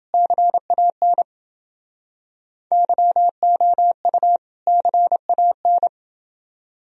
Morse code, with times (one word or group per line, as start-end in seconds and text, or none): 0.1-1.2 CAN
2.7-4.4 YOU
4.7-5.9 CAN